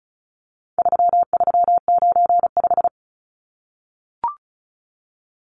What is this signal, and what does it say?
Morse code, heard as a keyed tone, about 34 wpm, 700 Hz. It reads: 3395
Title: Numbers - 4-Digit 35wpm